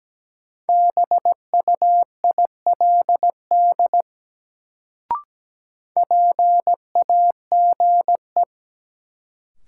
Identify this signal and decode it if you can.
Morse code, heard as a keyed tone, about 17 wpm, 700 Hz. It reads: BUILD PAGE
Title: Sentences from Top 1000 Words - Review Part 1 - New Word Drill - 17wpm